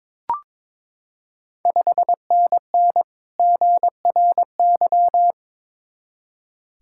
Morse code, heard as a keyed tone, about 23 wpm, 690 Hz. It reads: 5NN GRY
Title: Ontario QSO Party Exchange 22wpm